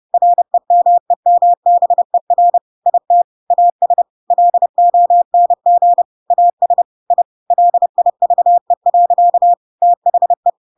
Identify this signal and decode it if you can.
Morse code, heard as a keyed tone, about 30 wpm, 700 Hz. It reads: REMEMBER IT AS LONG AS I LIVE. THE
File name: Pinocchio / Pinocchio ch7 30 30 ik1hge